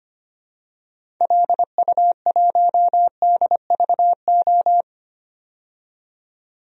Morse code, heard as a keyed tone, about 25 wpm, 700 Hz. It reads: LU1DVO
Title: DXCC-100 Call Signs - 6 Characters - 25wpm